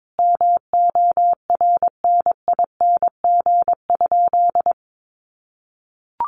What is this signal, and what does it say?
Morse code, heard as a keyed tone, about 22 wpm, 700 Hz. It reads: MORNING?
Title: Sentences from Top 1300 Words 22wpm